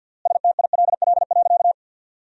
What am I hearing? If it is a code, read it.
Morse code, heard as a keyed tone, about 53 wpm, 700 Hz. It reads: STILL.
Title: 3x Word-Spacing - Sentences from Top 500 Words - 50wpm